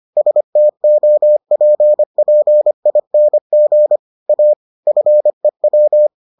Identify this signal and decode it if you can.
Morse code, heard as a keyed tone, about 25 wpm, 600 Hz. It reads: STOPPING A FEW